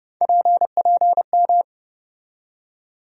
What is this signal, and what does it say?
Morse code, heard as a keyed tone, about 30 wpm, 700 Hz. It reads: PPM
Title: USA Call Sign Postfixes - 3 Letters 30wpm